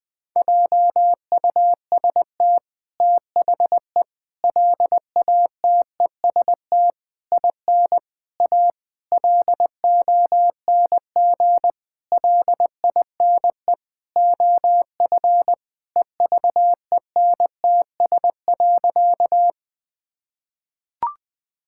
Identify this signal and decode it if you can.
Morse code, heard as a keyed tone, about 19 wpm, 700 Hz. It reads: JUST THE LATEST IN A LONG LINE OF EVENTS.
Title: Sentences with 8-Word Phrases - Spoken Beginning - 20wpm